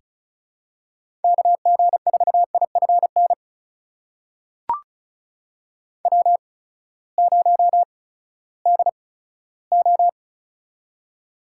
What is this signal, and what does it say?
Morse code, heard as a keyed tone, about 36 wpm, spaced as for 11 wpm, 700 Hz. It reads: KG4IF NW0DO
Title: USA Call Signs - 8x Character Spacing - 35wpm